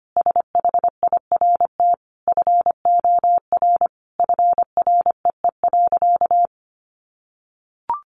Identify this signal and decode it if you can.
Morse code, heard as a keyed tone, about 25 wpm, 700 Hz. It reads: SHIRT FOR FREE.
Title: Additional Easy Phrases 25wpm